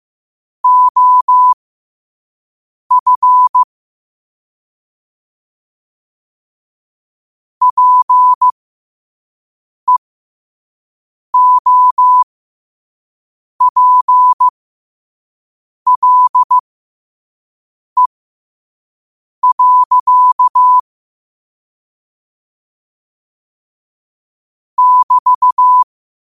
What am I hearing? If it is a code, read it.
Morse code, heard as a keyed tone, about 15 wpm, spaced as for 5 wpm, 1000 Hz. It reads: OF PEOPLE. <BT>
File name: qotdfiles / 5wpm